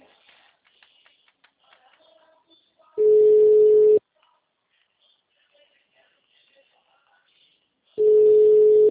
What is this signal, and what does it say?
Ringback tone (Europe, 425 Hz), with speech.
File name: out- -2039- - - .63